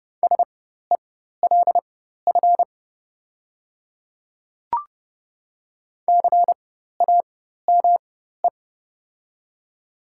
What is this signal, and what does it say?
Morse code, heard as a keyed tone, about 30 wpm, spaced as for 14 wpm, 700 Hz. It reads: SELF CAME
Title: 4x Character Spacing - Top 500 Words - 30wpm